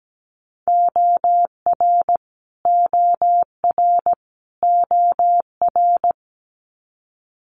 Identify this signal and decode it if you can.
Morse code, heard as a keyed tone, about 17 wpm, 700 Hz. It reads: OR OR OR
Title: Multi-letter Phonemes Repeated 3x 17wpm